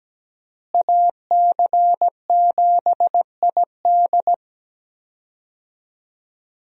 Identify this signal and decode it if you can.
Morse code, heard as a keyed tone, about 17 wpm, 700 Hz. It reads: AC7ID
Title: USA Call Signs - 5 Characters - Repeated 2x - 17wpm